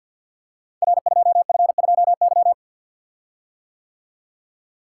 Morse code, heard as a keyed tone, about 50 wpm, 700 Hz. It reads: R2F2Y
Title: Code Groups - 5 Letter:Numbers - 50wpm